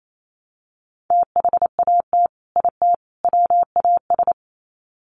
Morse code, heard as a keyed tone, about 28 wpm, 700 Hz. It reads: THAT IT WAS
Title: Sets of 3 Words 28wpm